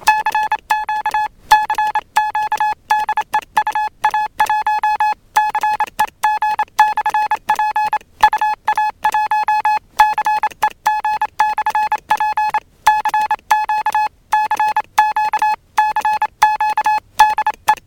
Morse code, heard as a keyed tone, about 27 wpm, 850 Hz. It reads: CQ CQ DE UA1CEG/P UA1CEG/P CQ CQ CQ DE